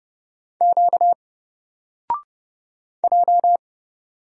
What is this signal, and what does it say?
Morse code, heard as a keyed tone, about 29 wpm, 700 Hz. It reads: Q J
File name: Single-Letter-Number-30wpm